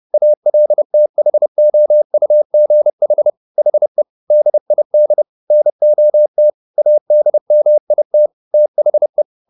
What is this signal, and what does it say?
Morse code, heard as a keyed tone, about 30 wpm, 600 Hz. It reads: ALTHOUGH HE DID NOT ADMIT THE
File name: the old man and the sea / the old man and the sea 1 30wpm